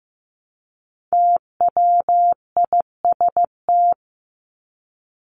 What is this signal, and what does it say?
Morse code, heard as a keyed tone, about 15 wpm, 700 Hz. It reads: TWIST
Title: Top 1751-2000 Words 15wpm